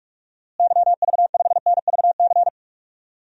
Morse code, heard as a keyed tone, about 46 wpm, 700 Hz. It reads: YV5NVC
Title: DXCC-100 Call Signs - 6 Characters - Repeated 3x - 45wpm